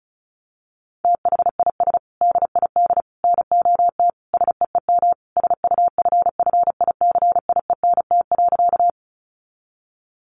Morse code, heard as a keyed tone, about 34 wpm, 700 Hz. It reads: THIS DID NOT SEEM SUFFICIENT.
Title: Sentences from Top 1400 Words 35wpm